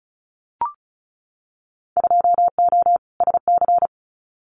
Morse code, heard as a keyed tone, about 36 wpm, 700 Hz. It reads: 2O SC
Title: Winter Field Day Exchanges 35wpm